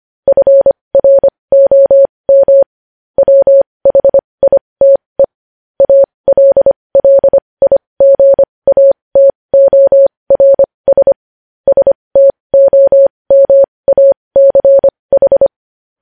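Morse code, heard as a keyed tone, about 25 wpm, spaced as for 20 wpm, 550 Hz. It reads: FROM WHITE ALLIGATORS STOMACH
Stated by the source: Morse practice of news headlines at 20 wpm.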